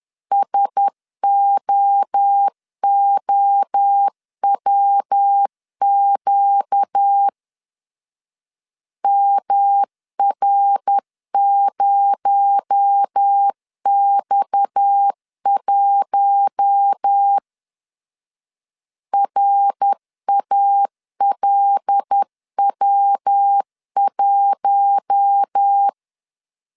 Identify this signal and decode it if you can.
Morse code, heard as a keyed tone, about 10 wpm, 800 Hz. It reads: SOOWQ MR0X1 RALW1
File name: CW / DET / 1 6 1500C S B 10wpm 4137